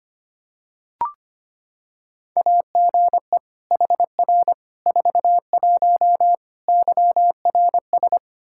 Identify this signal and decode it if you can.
Morse code, heard as a keyed tone, about 25 wpm, 700 Hz. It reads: AGE HR 41 YRS
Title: Practice QSO - Exchange 3 - More Info - 25wpm